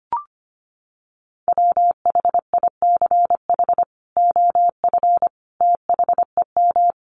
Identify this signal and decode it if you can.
Morse code, heard as a keyed tone, about 25 wpm, 700 Hz. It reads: WHICH OF THEM
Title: Sentences from Top 1100 Words 25wpm